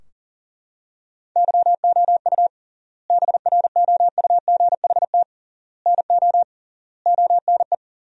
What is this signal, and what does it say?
Morse code, heard as a keyed tone, about 40 wpm, 700 Hz. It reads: YOU BROUGHT NO ONE
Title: 3x Word-Spacing - Sentences from Top 500 Words - 40wpm